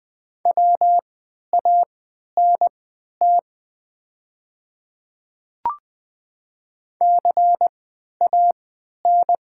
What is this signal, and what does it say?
Morse code, heard as a keyed tone, about 20 wpm, spaced as for 11 wpm, 700 Hz. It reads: WANT CAN
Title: 3x Character Spacing - Top 200 Words - 20wpm